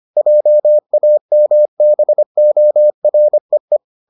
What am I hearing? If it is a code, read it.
Morse code, heard as a keyed tone, about 25 wpm, 600 Hz. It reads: JAMBOREE